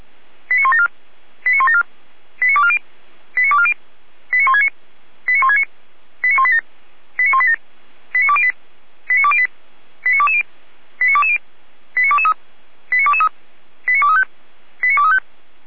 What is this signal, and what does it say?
Five-tone selective call (ZVEI), heard as quick runs of five tones, tones of about 70 ms. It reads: 89174 89174 89240 89240 89169 89169 89187 89187 89208 89208 89220 89220 89223 89223 89235 89235
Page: /455/SDR/ZVEI/89168/